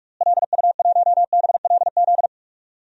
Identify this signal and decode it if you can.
Morse code, heard as a keyed tone, about 41 wpm, 700 Hz. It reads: LU1BLZ
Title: DXCC-100 Call Signs - 6 Characters - Repeated 3x - 45wpm